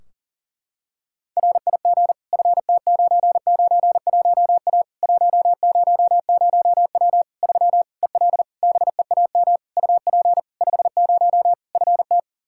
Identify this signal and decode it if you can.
Morse code, heard as a keyed tone, about 40 wpm, 700 Hz. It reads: RIG FT991A 100W 3 EL BEAM UP 50 FT
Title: K1USN SST QSO Practice 40wpm